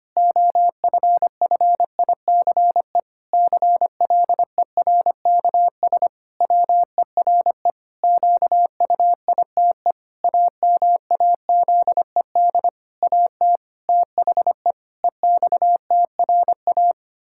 Morse code, heard as a keyed tone, about 25 wpm, 700 Hz. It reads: OFFICE CLERKS WERE QUITE AMAZED AT THE EXTRA
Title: Pangram Challenge v2 - 25wpm